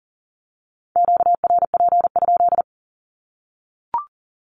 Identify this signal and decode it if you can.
Morse code, heard as a keyed tone, about 40 wpm, 700 Hz. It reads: QRP?
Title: Q-Codes 40wpm